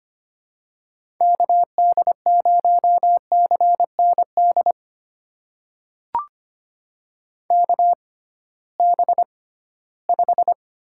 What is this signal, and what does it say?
Morse code, heard as a keyed tone, about 25 wpm, spaced as for 16 wpm, 700 Hz. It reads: KD0CND KB5
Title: USA Call Signs - 6 Characters - 6x Character Spacing - 25wpm